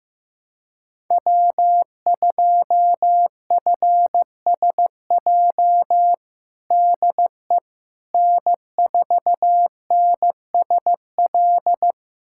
Morse code, heard as a keyed tone, about 15 wpm, 700 Hz. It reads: W2FSJ DE N4NSL
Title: USA Callsign DE Callsign - 5 Characters - 15wpm